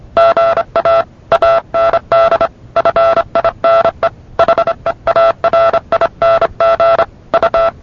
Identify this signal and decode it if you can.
Morse code, heard as a keyed tone, about 27 wpm, 670 Hz. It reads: GA AND FINE HEARING U